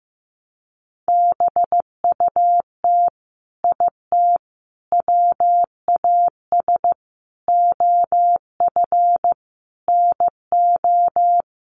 Morse code, heard as a keyed tone, about 15 wpm, 700 Hz. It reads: BUT IT WAS OF NO
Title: Sentences from Top 100 Words 15wpm